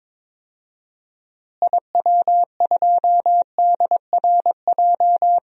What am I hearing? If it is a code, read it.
Morse code, heard as a keyed tone, about 22 wpm, 700 Hz. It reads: IW2DRJ
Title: DXCC-100 Call Signs - 6 Characters - Repeated 3x - 22wpm